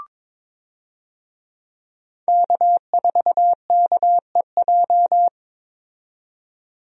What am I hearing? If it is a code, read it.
Morse code, heard as a keyed tone, about 22 wpm, 700 Hz. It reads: K4KEJ
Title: USA Call Signs - 5 Characters - 22wpm